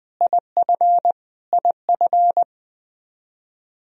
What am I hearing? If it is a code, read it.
Morse code, heard as a keyed tone, about 20 wpm, 700 Hz. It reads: IF IF
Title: Top 100 Words Repeated 20wpm